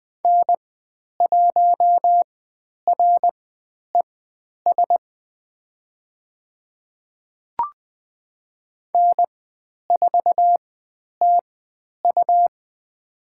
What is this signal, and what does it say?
Morse code, heard as a keyed tone, about 20 wpm, spaced as for 10 wpm, 700 Hz. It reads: N1RES N4TU
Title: 102 Lesson 17g - Letter u - Call Signs 20wpm